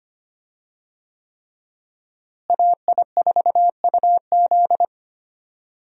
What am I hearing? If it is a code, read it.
Morse code, heard as a keyed tone, about 25 wpm, 700 Hz. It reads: AI4UZ
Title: USA Call Signs - 5 Characters - Repeated 3x - 25wpm